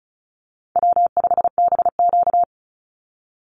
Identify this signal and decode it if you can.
Morse code, heard as a keyed tone, about 34 wpm, 700 Hz. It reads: W5BQ
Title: Easy Call Signs Repeated 3x - 35wpm